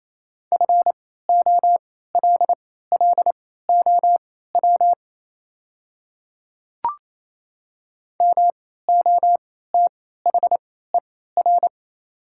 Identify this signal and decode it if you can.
Morse code, heard as a keyed tone, about 28 wpm, spaced as for 16 wpm, 700 Hz. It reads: FOLLOW MOTHER
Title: 3x Character Spacing - Top 200 Words - 28wpm